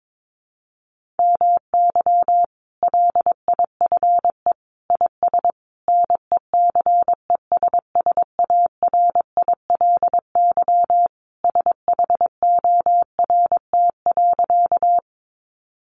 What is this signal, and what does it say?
Morse code, heard as a keyed tone, about 22 wpm, 700 Hz. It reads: MY LIFE IS NECESSARILY SHORT.
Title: Sentences from Top 1400 Words 22wpm